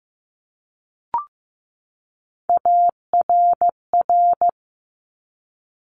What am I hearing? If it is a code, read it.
Morse code, heard as a keyed tone, about 15 wpm, 700 Hz. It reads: ARR
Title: Top Three Letter Combinations in Words 15wpm